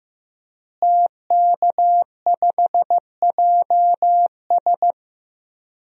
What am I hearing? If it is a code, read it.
Morse code, heard as a keyed tone, about 15 wpm, 700 Hz. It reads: TK5JS
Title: DXCC-100 Call Signs - 5 Characters - Repeated 3x - 15wpm